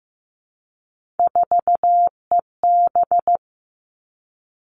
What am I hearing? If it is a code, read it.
Morse code, heard as a keyed tone, about 15 wpm, 700 Hz. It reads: 4EB
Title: All Cut Numbers - 3-Digit 15wpm